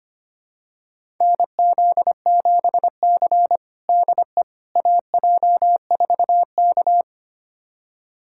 Morse code, heard as a keyed tone, about 25 wpm, 700 Hz. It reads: NZ7C DE AJ4K
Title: USA Callsign DE Callsign - 4 Characters - 25wpm